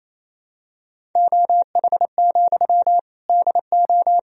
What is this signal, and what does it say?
Morse code, heard as a keyed tone, about 28 wpm, 700 Hz. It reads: OH, DO
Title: Sentences from Top 500 Words and Abbreviations 28wpm